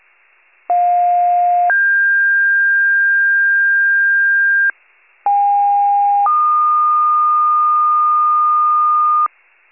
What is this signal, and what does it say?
Two-tone page, heard as a one-second tone, then a three-second tone, about 710 then 1600 Hz, about 810 then 1220 Hz, with noise. It is heard from 0.7 to 9.3 s.
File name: wcalert